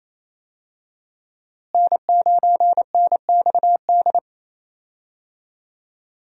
Morse code, heard as a keyed tone, about 28 wpm, 700 Hz. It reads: N9NXD
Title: USA Call Signs - 5 Characters - Repeated 2x - 28wpm